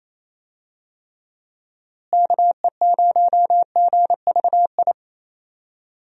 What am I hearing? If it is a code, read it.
Morse code, heard as a keyed tone, about 28 wpm, 700 Hz. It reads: KE0GVI
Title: USA Call Signs - 6 Characters - Repeated 2x - 28wpm